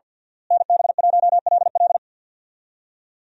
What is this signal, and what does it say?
Morse code, heard as a keyed tone, about 52 wpm, 700 Hz. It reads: NB1LL
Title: USA Call Signs - 5 Characters - Repeated 3x - 50wpm